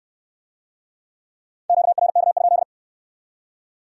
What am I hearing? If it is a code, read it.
Morse code, heard as a keyed tone, about 63 wpm, 700 Hz. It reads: QRK?